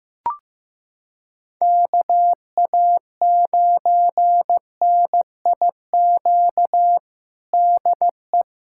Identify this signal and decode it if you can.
Morse code, heard as a keyed tone, about 15 wpm, 700 Hz. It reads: KA9NIQ DE
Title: USA Callsign DE Callsign - 6 Characters - 15wpm